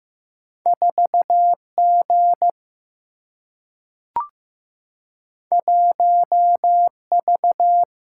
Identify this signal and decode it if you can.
Morse code, heard as a keyed tone, about 15 wpm, 700 Hz. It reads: 4 G 1 V
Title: Single-Letter Single-Number 15wpm